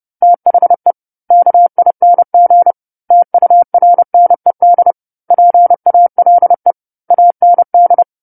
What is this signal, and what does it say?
Morse code, heard as a keyed tone, about 30 wpm, 700 Hz. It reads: THE KING TURNED PALE AND
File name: 10-Chapter-10 30 WPM